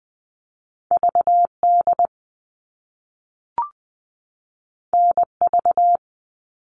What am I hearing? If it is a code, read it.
Morse code, heard as a keyed tone, about 20 wpm, 700 Hz. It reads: V D N V
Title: All Cut Numbers - 2-Digit 20wpm